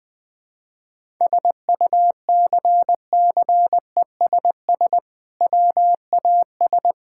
Morse code, heard as a keyed tone, about 20 wpm, 700 Hz. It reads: SUCCESS WAS
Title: Sentences from Top 1000 Words - Review Part 6 - 20wpm